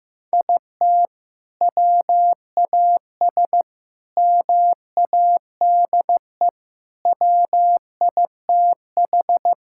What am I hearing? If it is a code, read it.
Morse code, heard as a keyed tone, about 16 wpm, 700 Hz. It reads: IT WAS MADE WITH
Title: Sentences from Top 1000 Words - Review Part 5 - 15wpm